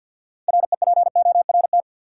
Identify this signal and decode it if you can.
Morse code, heard as a keyed tone, about 52 wpm, 700 Hz. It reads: REPORT
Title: Six Letter Words - 50wpm